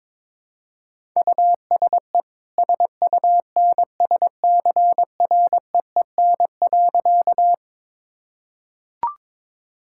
Morse code, heard as a keyed tone, about 22 wpm, 690 Hz. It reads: USE SUNSCREEN.
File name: practice / 22wpm / Easy Sentences with Hints - 22wpm